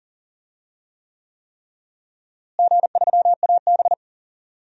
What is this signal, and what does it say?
Morse code, heard as a keyed tone, about 40 wpm, 700 Hz. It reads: G3AB
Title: DXCC-100 Call Signs - 4 Characters - Repeated 3x - 40wpm